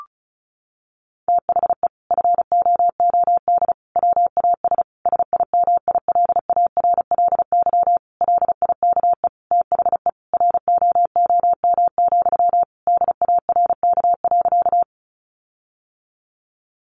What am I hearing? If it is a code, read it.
Morse code, heard as a keyed tone, about 34 wpm, 700 Hz. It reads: THE FOOD WAS SIMILARLY LIKE THE ROOM, DARK.
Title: Sentences from Top 1400 Words 35wpm